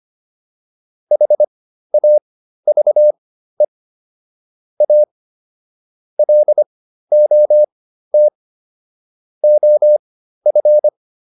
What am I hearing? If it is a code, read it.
Morse code, heard as a keyed tone, about 25 wpm, spaced as for 13 wpm, 600 Hz. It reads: HAVE A LOT OF
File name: WD302 13